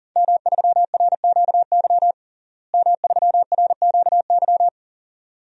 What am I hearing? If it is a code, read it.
Morse code, heard as a keyed tone, about 40 wpm, 700 Hz. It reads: M3RQY M3RQY
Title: DXCC-100 Call Signs - 5 Characters - Repeated 2x - 40wpm